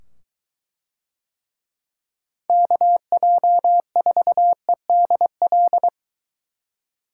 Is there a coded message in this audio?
Yes: Morse code from 2.5 to 5.9 s, heard as a keyed tone, about 22 wpm, 700 Hz. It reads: KJ4EDL